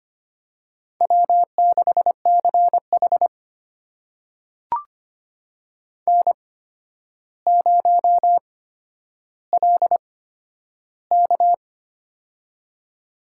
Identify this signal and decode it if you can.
Morse code, heard as a keyed tone, about 25 wpm, spaced as for 11 wpm, 700 Hz. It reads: W6CH N0LK
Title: USA Call Signs - 4 Characters - 8x Character Spacing - 25wpm